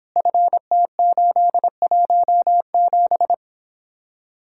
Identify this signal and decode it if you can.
Morse code, heard as a keyed tone, about 31 wpm, 700 Hz. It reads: FT817